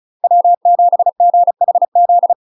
Morse code, heard as a keyed tone, about 36 wpm, 700 Hz. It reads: W7GHZ
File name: UK MASTER QSOs TRAINING / 35WPM / UK MASTER QSOs 0026